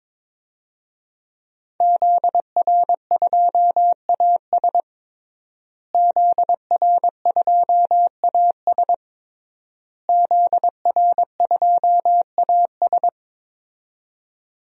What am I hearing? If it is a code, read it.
Morse code, heard as a keyed tone, about 21 wpm, 700 Hz. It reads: ZR2AS ZR2AS ZR2AS
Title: DXCC-100 Call Signs - 5 Characters - Repeated 3x - 22wpm